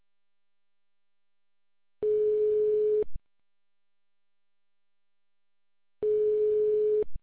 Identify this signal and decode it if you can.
Telephone call progress tone (425 Hz, 1 s on, 3 s off), heard as a telephone network's tone, about 420 Hz. Ringback tone.